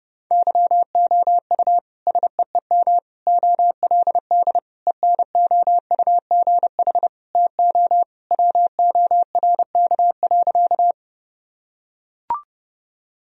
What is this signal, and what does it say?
Morse code, heard as a keyed tone, about 30 wpm, 700 Hz. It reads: YOU SEEM OLD ENOUGH TO WORK.
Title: Sentences from Top 1000 Words - Review Part 1 - 30wpm